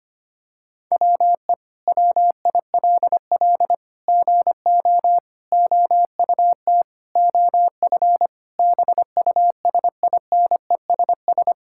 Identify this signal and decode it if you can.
Morse code, heard as a keyed tone, about 25 wpm, 700 Hz. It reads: WE WILL GO OUT OF BUSINESS
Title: Sentences from Top 1000 Words - Review Part 6 - 25wpm